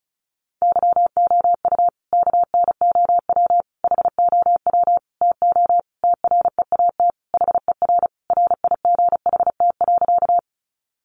Morse code, heard as a keyed tone, about 34 wpm, 700 Hz. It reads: YOU KNOW HOW TO TREAT HER RIGHT.
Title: Sentences from Top 1300 Words 35wpm